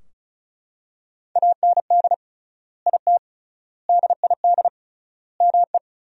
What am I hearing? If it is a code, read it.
Morse code, heard as a keyed tone, about 36 wpm, 700 Hz. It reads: AND IT DID ME
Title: 3x Word-Spacing - Sentences from Top 200 Words - 35wpm